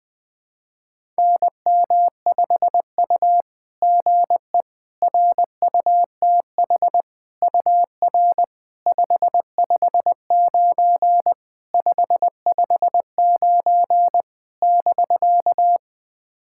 Morse code, heard as a keyed tone, about 20 wpm, 690 Hz. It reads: NM5U GE RUTH UR 559 559 <BK>
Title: SOTA QSO Practice 20wpm